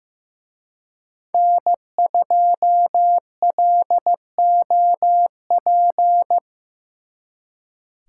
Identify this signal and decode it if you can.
Morse code, heard as a keyed tone, about 18 wpm, 700 Hz. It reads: N2LOP
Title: Speed Racing - USA Call Signs 15wpm